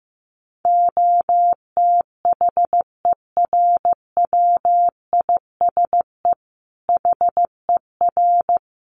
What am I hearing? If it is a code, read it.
Morse code, heard as a keyed tone, about 15 wpm, 700 Hz. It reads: OTHERWISE HER
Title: Sentences from Top 1200 Words 15wpm